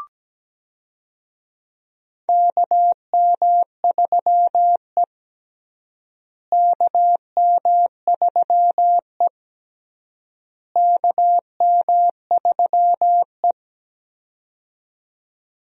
Morse code, heard as a keyed tone, about 17 wpm, 700 Hz. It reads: KM3E KM3E KM3E
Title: USA Call Signs - 4 Characters - Repeated 3x - 17wpm